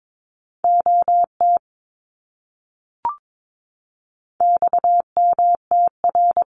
Morse code, heard as a keyed tone, about 22 wpm, 700 Hz. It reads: OT XMTR
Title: Abbreviations Repeated 3x - 22wpm